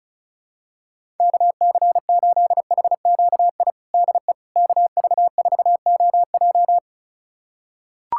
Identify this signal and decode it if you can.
Morse code, heard as a keyed tone, about 36 wpm, 700 Hz. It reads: KC8HQI DE KV4OJ
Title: USA Callsign DE Callsign 35wpm